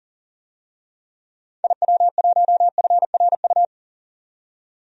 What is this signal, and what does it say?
Morse code, heard as a keyed tone, about 40 wpm, 700 Hz. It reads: IW1FRU
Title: DXCC-100 Call Signs - 6 Characters - 40wpm